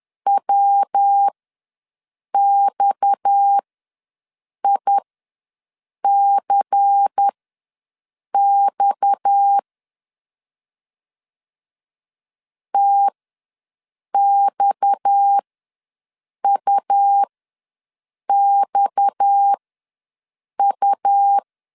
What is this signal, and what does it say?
Morse code, heard as a keyed tone, about 10 wpm, spaced as for 6 wpm, 800 Hz. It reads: WXICX TXUXU
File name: LEZIONI / 6 PKX0 / 6 PKX ws 10wpm 3706